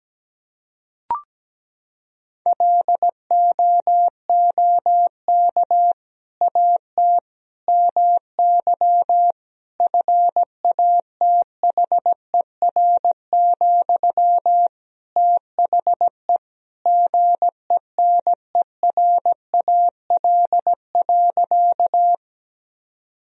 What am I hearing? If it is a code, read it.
Morse code, heard as a keyed tone, about 17 wpm, 700 Hz. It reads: LOOK AT MY FATHER, THE GENERAL.
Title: Sentences from Top 1000 Words - Review Part 3 - 17wpm